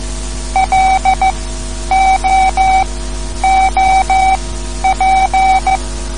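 Morse code, heard as a keyed tone, about 14 wpm, spaced as for 10 wpm, 760 Hz. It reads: LOOP
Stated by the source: Morse code lesson, third lesson in a series for learning the Morse alphabet.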